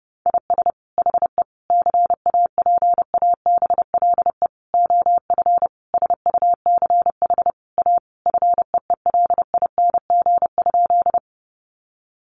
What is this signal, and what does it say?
Morse code, heard as a keyed tone, about 30 wpm, 700 Hz. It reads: IS HE CAPABLE OF SUCH A FEELING?
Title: Sentences from Top 1200 Words 30wpm